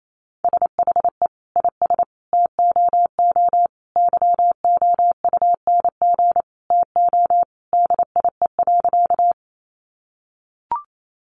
Morse code, heard as a keyed tone, about 28 wpm, 700 Hz. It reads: SHE IS TOO YOUNG TO DIE.
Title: Sentences from Top 1300 Words 28wpm